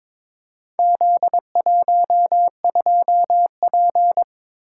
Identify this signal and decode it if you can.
Morse code, heard as a keyed tone, about 22 wpm, 700 Hz. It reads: Z12P
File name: practice / 22wpm / Four Letter-Number Combinations - 22wpm